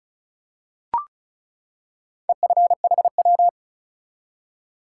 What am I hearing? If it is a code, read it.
Morse code, heard as a keyed tone, about 36 wpm, 700 Hz. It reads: EFHW
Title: Top 180 QSO Elements 35wpm